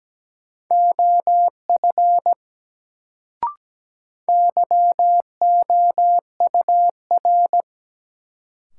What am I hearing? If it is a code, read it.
Morse code, heard as a keyed tone, about 17 wpm, 700 Hz. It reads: OF YOUR
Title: Sentences from Top 100 Words - New Word Drill - 17wpm